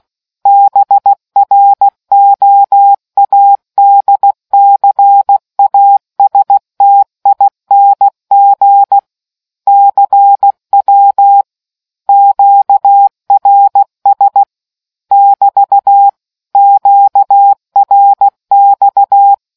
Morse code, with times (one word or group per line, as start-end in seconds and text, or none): 0.5-9.0 BROADCASTING
9.7-11.4 CW
12.1-14.4 QRS
15.1-16.1 <BT>
16.6-19.3 QRX